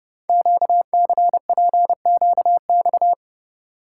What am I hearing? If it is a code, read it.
Morse code, heard as a keyed tone, about 30 wpm, 700 Hz. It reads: QCPQX
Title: Code Groups - 5 Letters - 30wpm